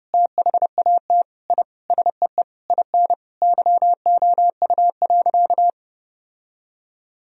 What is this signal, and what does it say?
Morse code, heard as a keyed tone, about 30 wpm, 700 Hz. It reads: THAT I SEE IN YOU.
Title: Sentences from Top 1000 Words - Review Part 2 - 30wpm